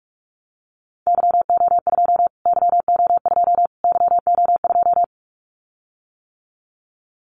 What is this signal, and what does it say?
Morse code, heard as a keyed tone, about 44 wpm, 700 Hz. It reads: YO2 YO2 YO2
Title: Callsign Prefixes for Top 25 DXCC - Repeated 3x - 45wpm